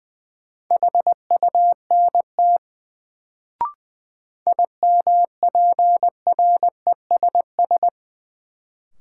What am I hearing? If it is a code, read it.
Morse code, heard as a keyed tone, about 20 wpm, 700 Hz. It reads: HUNT IMPRESS
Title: Sentences from Top 1600 Words - New Word Drill - 20wpm